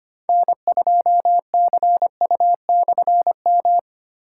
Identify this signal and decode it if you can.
Morse code, heard as a keyed tone, about 25 wpm, 700 Hz. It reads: N2CU/M
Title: K1USN SST QSO Practice - 25wpm